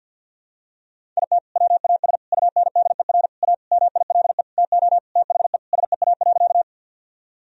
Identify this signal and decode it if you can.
Morse code, heard as a keyed tone, about 48 wpm, 700 Hz. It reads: IT WAS UNDER A MILE TO THE SEA.
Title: Sentences from Top 1000 Words - Review Part 1 - 50wpm